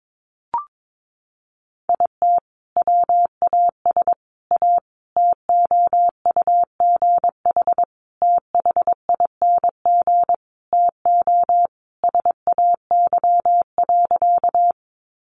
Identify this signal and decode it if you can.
Morse code, heard as a keyed tone, about 22 wpm, 700 Hz. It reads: IT WAS A TOUGH THING TO SAY.
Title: Sentences from Top 1200 Words 22wpm